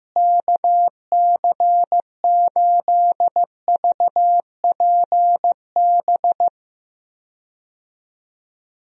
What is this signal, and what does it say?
Morse code, heard as a keyed tone, about 15 wpm, 700 Hz. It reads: KC8VPB
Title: USA Call Signs - 6 Characters - Repeated 2x - 15wpm